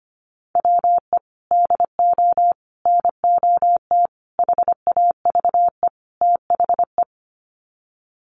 Morse code, heard as a keyed tone, about 25 wpm, 700 Hz. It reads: WE DO NOT HAVE THE
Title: Sets of 5 Words 25wpm